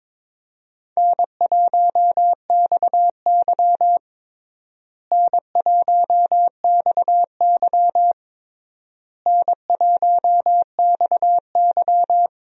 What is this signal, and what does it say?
Morse code, heard as a keyed tone, about 21 wpm, 700 Hz. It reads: N1XY N1XY N1XY
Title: USA Call Signs - 4 Characters - Repeated 3x - 22wpm